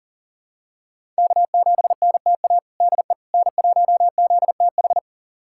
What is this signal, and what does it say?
Morse code, heard as a keyed tone, about 40 wpm, 700 Hz. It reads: K7NTA DE N1ZTH
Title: USA Callsign DE Callsign - 5 Characters - 40wpm